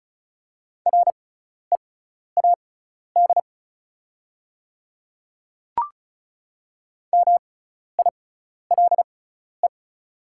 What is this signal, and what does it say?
Morse code, heard as a keyed tone, about 36 wpm, spaced as for 12 wpm, 700 Hz. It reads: READ MILE
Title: 6x Character Spacing - Top 300 Words - 35wpm